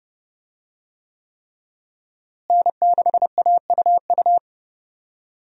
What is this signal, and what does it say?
Morse code, heard as a keyed tone, about 32 wpm, 700 Hz. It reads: N6AUU
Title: USA Call Signs - 5 Characters - Repeated 3x - 30wpm